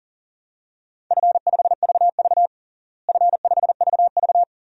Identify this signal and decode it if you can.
Morse code, heard as a keyed tone, about 40 wpm, 700 Hz. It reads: F5VV F5VV
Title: DXCC-100 Call Signs - 4 Characters - Repeated 3x - 40wpm